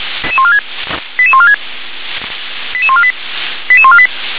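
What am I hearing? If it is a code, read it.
Five-tone selective call (ZVEI), heard as quick runs of five tones, tones of about 70 ms. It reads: 88146 88148 88148
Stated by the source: Five-tone ZVEI selective call received on SDR.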